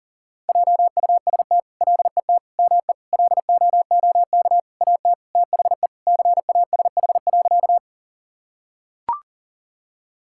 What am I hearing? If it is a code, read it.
Morse code, heard as a keyed tone, about 40 wpm, 700 Hz. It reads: JUST LET ME LOOK AT THE CASH.
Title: Sentences from Top 1000 Words - Review Part 5 - 40wpm